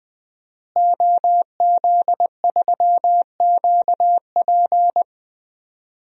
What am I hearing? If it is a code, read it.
Morse code, heard as a keyed tone, about 20 wpm, 700 Hz. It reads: OZ3QP
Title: DXCC-100 Call Signs - 5 Characters - Repeated 3x - 20wpm